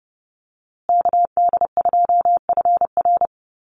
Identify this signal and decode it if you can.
Morse code, heard as a keyed tone, about 28 wpm, 700 Hz. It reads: KD2FR
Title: USA Call Signs Repeated 2x 30wpm